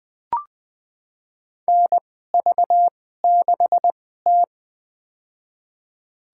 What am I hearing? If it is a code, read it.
Morse code, heard as a keyed tone, about 20 wpm, spaced as for 14 wpm, 700 Hz. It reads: NV6T
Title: USA Call Signs - 2x Character Spacing - 20wpm